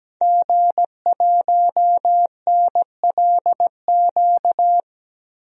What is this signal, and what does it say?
Morse code, heard as a keyed tone, about 17 wpm, 700 Hz. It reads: G1NLQ
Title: DXCC-100 Call Signs - 5 Characters - 17wpm